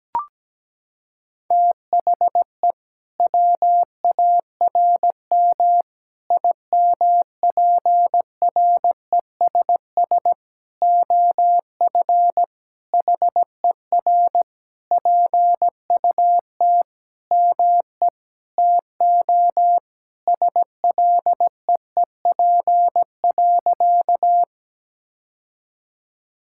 Morse code, heard as a keyed tone, about 17 wpm, 700 Hz. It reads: THE WARM IMPRESS OF HER PUT ME TO SLEEP.
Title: Sentences from Top 1600 Words 17wpm